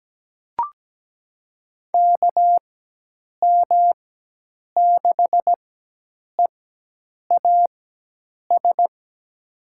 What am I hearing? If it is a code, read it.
Morse code, heard as a keyed tone, about 17 wpm, spaced as for 8 wpm, 700 Hz. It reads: KM6EAS
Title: USA Call Signs - 6 Characters - 4x Character Spacing - 17wpm